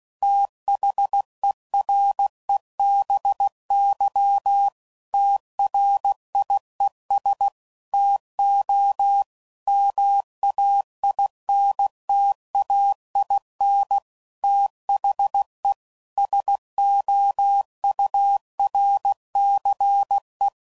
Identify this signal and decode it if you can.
Morse code, heard as a keyed tone, about 16 wpm, 800 Hz. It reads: THEREBY TRIES TO MAINTAIN THE SOURCE
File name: CW Code Groups @15WPM / Technical Subjects / XC